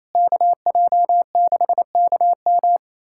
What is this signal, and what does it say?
Morse code, heard as a keyed tone, about 27 wpm, 700 Hz. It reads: KJ6KM
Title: USA Callsign DE Callsign - 5 Characters - 28wpm